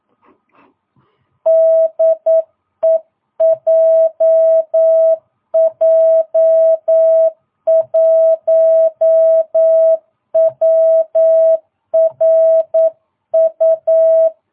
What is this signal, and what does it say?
Morse code, heard as a keyed tone, about 9 wpm, 650 Hz. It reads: DE JJ1WRU
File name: jh8cbh / 2022cw